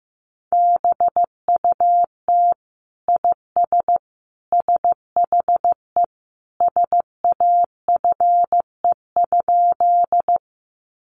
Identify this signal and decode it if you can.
Morse code, heard as a keyed tone, about 14 wpm, 700 Hz. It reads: BUT IS SHE SAFE?
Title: Sentences from Top 900 Words 15wpm